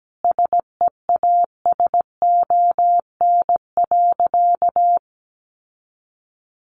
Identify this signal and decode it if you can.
Morse code, heard as a keyed tone, about 17 wpm, 700 Hz. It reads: SEASON.
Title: Sentences from Top 1400 Words 17wpm